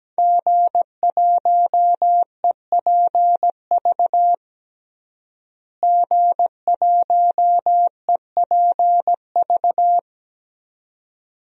Morse code, heard as a keyed tone, about 17 wpm, 700 Hz. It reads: G1EPV G1EPV
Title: DXCC-100 Call Signs - 5 Characters - Repeated 2x - 17wpm